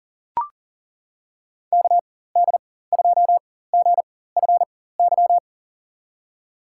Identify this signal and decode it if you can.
Morse code, heard as a keyed tone, about 40 wpm, spaced as for 19 wpm, 700 Hz. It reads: KD2GFY
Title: USA Call Signs - 6 Characters - 4x Character Spacing - 40wpm